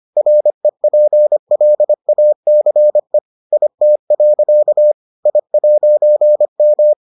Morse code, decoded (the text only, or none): REPLACE IT. I'M